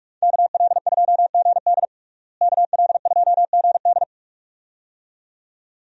Morse code, heard as a keyed tone, about 46 wpm, 700 Hz. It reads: KL2GD KL2GD
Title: USA Call Signs - 5 Characters - Repeated 2x - 45wpm